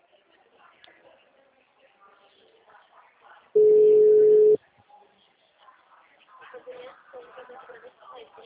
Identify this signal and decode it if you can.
Ringback tone (Europe, 425 Hz), with speech.